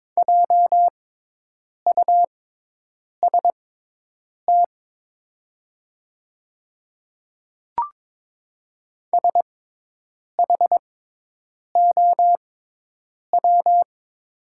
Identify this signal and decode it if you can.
Morse code, heard as a keyed tone, about 22 wpm, spaced as for 8 wpm, 700 Hz. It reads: JUST SHOW
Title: 6x Character Spacing - Top 300 Words - 22wpm